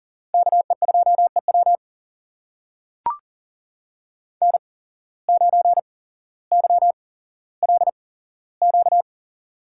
Morse code, heard as a keyed tone, about 39 wpm, spaced as for 17 wpm, 700 Hz. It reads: KE2EW N9YLQ
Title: USA Call Signs - 5 Characters - 8x Character Spacing - 40wpm